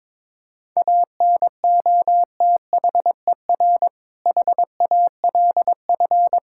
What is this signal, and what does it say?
Morse code, heard as a keyed tone, about 22 wpm, 700 Hz. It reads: ANOTHER HALF
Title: Sentences from Top 1600 Words 22wpm